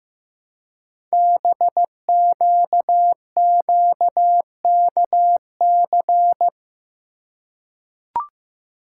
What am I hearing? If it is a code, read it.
Morse code, heard as a keyed tone, about 15 wpm, 700 Hz. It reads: BQQKC
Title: Code Groups - 5 Letters - 15wpm